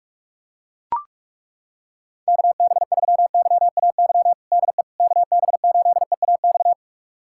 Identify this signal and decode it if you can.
Morse code, heard as a keyed tone, about 41 wpm, 700 Hz. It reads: KB3YAY DE KB8EAX
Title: USA Callsign DE Callsign - 6 Characters - 45wpm